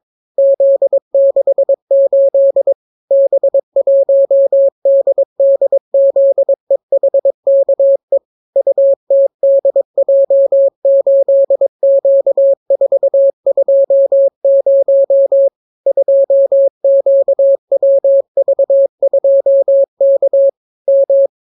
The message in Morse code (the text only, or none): Z68 B1DDZEHKE UTDJ8Q420 2QWV2K M